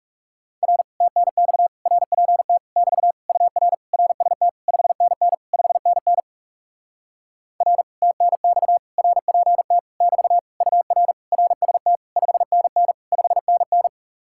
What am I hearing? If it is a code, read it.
Morse code, heard as a keyed tone, about 37 wpm, 700 Hz. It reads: R TNX RPT <BT> UR RST 5NN 5NN R TNX RPT <BT> UR RST 5NN 5NN
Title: Speed-Racing Common QSO Phrases - 30wpm